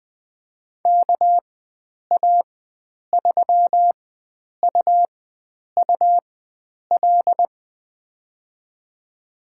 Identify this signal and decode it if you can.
Morse code, heard as a keyed tone, about 20 wpm, spaced as for 9 wpm, 700 Hz. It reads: KA3UUL
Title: USA Call Signs - 6 Characters - 4x Character Spacing - 20wpm